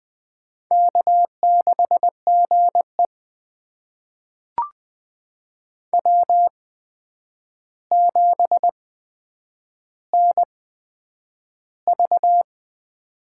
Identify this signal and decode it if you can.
Morse code, heard as a keyed tone, about 21 wpm, spaced as for 9 wpm, 700 Hz. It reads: K6GE W7NV
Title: USA Call Signs - 4 Characters - 8x Character Spacing - 20wpm